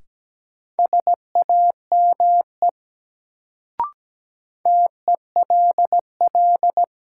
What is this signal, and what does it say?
Morse code, heard as a keyed tone, about 17 wpm, 700 Hz. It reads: SAME TELL
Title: Sets of 2 Words - New Word Drill - 17wpm